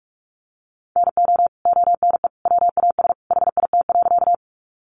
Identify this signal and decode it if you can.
Morse code, heard as a keyed tone, about 44 wpm, 700 Hz. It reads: NO ONE WAS HIT.
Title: Sentences from Top 1200 Words 45wpm